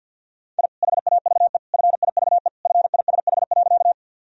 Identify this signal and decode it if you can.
Morse code, heard as a keyed tone, about 51 wpm, 700 Hz. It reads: I HAVE FIVE FISH.
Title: Sentences from Top 1000 Words - Review Part 2 - 50wpm